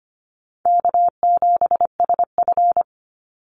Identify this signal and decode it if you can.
Morse code, heard as a keyed tone, about 25 wpm, 700 Hz. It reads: K7SF
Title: Easy Call Signs - 25wpm